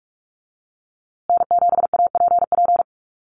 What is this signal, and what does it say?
Morse code, heard as a keyed tone, about 44 wpm, 700 Hz. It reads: N7APP
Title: USA Call Signs Repeated 2x 45wpm